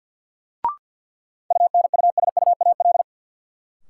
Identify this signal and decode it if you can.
Morse code, heard as a keyed tone, about 52 wpm, 700 Hz. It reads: UNUSUAL
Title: Sentences from Top 1400 Words - New Word Drill Alt - 50wpm